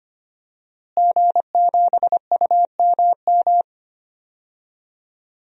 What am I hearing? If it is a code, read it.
Morse code, heard as a keyed tone, about 25 wpm, 700 Hz. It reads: G7UMM
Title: DXCC-100 Call Signs - 5 Characters - Repeated 2x - 25wpm